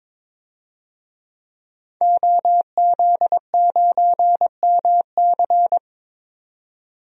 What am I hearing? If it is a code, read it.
Morse code, heard as a keyed tone, about 22 wpm, 700 Hz. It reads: OZ9MC
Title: DXCC-100 Call Signs - 5 Characters - 22wpm